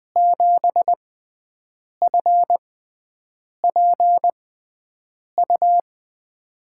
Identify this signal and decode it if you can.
Morse code, heard as a keyed tone, about 20 wpm, spaced as for 7 wpm, 700 Hz. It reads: 7FPU
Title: USA Call Signs - 6 Characters - 6x Character Spacing - 20wpm